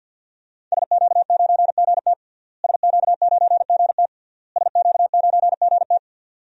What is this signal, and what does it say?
Morse code, heard as a keyed tone, about 51 wpm, 700 Hz. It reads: SQ9GT SQ9GT SQ9GT
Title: DXCC-100 Call Signs - 5 Characters - Repeated 3x - 50wpm